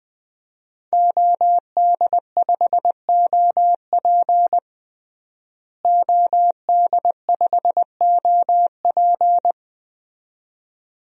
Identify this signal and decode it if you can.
Morse code, heard as a keyed tone, about 21 wpm, 700 Hz. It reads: OD5OP OD5OP
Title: DXCC-100 Call Signs - 5 Characters - Repeated 2x - 20wpm